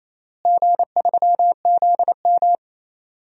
Morse code, heard as a keyed tone, about 27 wpm, 700 Hz. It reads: G3ZM
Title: DXCC-100 Call Signs - 4 Characters - Repeated 3x - 28wpm